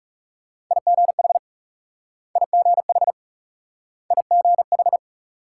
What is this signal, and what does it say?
Morse code, heard as a keyed tone, about 38 wpm, 700 Hz. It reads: IGH IGH IGH